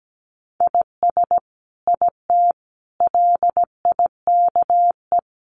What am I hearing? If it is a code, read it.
Morse code, heard as a keyed tone, about 17 wpm, 700 Hz. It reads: IS IT LIKE
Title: Sentences from Top 100 Words 17wpm